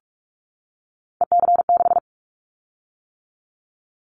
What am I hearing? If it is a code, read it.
Morse code, heard as a keyed tone, about 46 wpm, 700 Hz. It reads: EC6
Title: Callsign Prefixes for Top 100 DXCC - 45wpm